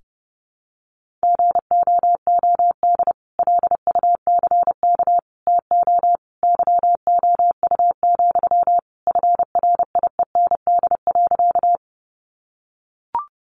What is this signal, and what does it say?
Morse code, heard as a keyed tone, about 30 wpm, 700 Hz. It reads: GOOD LUCK TO YOU, FRIEND.
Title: Sentences from Top 1500 Words 30wpm